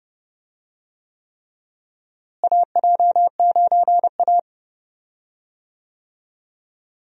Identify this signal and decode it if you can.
Morse code, heard as a keyed tone, about 31 wpm, 700 Hz. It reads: AJ9A
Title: USA Call Signs - 4 Characters - Repeated 3x - 30wpm